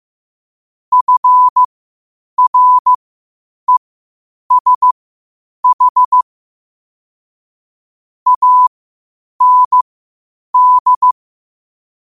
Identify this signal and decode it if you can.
Morse code, heard as a keyed tone, about 15 wpm, spaced as for 8 wpm, 1000 Hz. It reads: FRESH AND
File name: qotdfiles / 8wpm